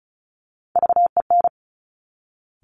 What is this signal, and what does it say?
Morse code, heard as a keyed tone, about 36 wpm, 700 Hz. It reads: VEN
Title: Top Three Letter Combinations in Words 35wpm